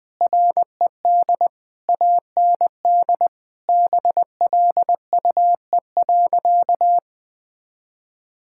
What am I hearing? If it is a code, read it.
Morse code, heard as a keyed tone, about 21 wpm, 700 Hz. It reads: RED AND BLUE.
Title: Sentences with 6-Word Phrases - 20wpm